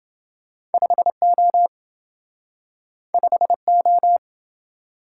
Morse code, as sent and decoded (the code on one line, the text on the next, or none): ..... --- / ..... ---
5O 5O